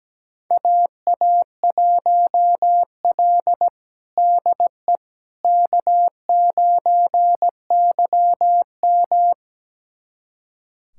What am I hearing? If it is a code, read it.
Morse code, heard as a keyed tone, about 17 wpm, 700 Hz. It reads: AA1L DE K9YM
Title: USA Callsign DE Callsign - 4 Characters - 17wpm